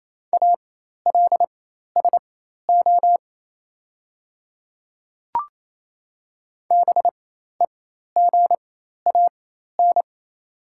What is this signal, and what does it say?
Morse code, heard as a keyed tone, about 28 wpm, spaced as for 13 wpm, 700 Hz. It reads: ALSO BEGAN
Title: 4x Character Spacing - Top 300 Words - 28wpm